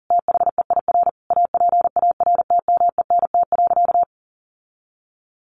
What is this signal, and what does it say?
Morse code, heard as a keyed tone, about 37 wpm, 700 Hz. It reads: THEIR APARTMENT.
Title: Sentences from Top 900 Words 40wpm